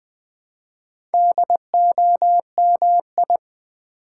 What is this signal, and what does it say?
Morse code, heard as a keyed tone, about 20 wpm, 700 Hz. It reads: DOMI
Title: Build A Word - 2 Letters At A Time - 8-Letter Words - 20wpm